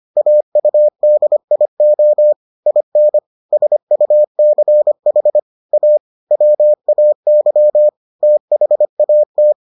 Morse code, heard as a keyed tone, about 25 wpm, 600 Hz. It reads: AUDIO IN SUCH A WAY THAT